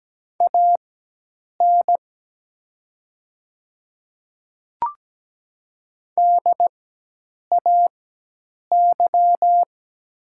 Morse code, heard as a keyed tone, about 17 wpm, spaced as for 8 wpm, 700 Hz. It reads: AN DAY
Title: 4x Character Spacing - Top 100 Words - 17wpm